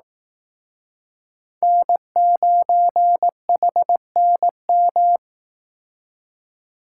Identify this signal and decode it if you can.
Morse code, heard as a keyed tone, about 19 wpm, 700 Hz. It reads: N9HNM
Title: Speed Racing - USA Call Signs 15wpm